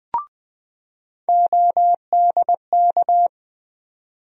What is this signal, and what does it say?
Morse code, heard as a keyed tone, about 20 wpm, 700 Hz. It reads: ODK
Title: USA Call Sign Postfixes - 3 Letters 20wpm